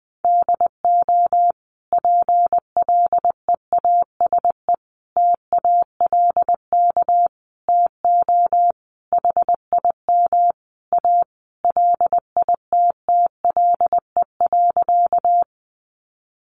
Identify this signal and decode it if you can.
Morse code, heard as a keyed tone, about 20 wpm, 700 Hz. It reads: DO PLEASE TALK TO HIM A LITTLE.
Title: Sentences from Top 1500 Words 20wpm